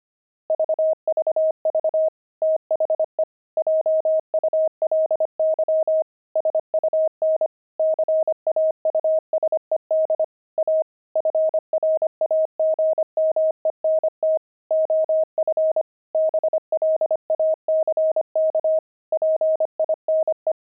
Morse code, heard as a keyed tone, about 25 wpm, 630 Hz. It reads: VVV THE JULY SUN CAUSED A FRAGMENT OF BLACK PINE
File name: pangram-fs-25-cs-25-fragment